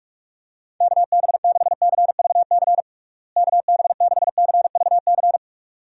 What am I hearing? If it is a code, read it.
Morse code, heard as a keyed tone, about 42 wpm, 700 Hz. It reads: KB6CVC KB6CVC
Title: USA Call Signs - 6 Characters - Repeated 2x - 45wpm